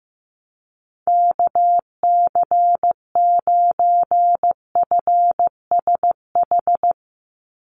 Morse code, heard as a keyed tone, about 15 wpm, 700 Hz. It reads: KC9FSH
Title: USA Call Signs 15wpm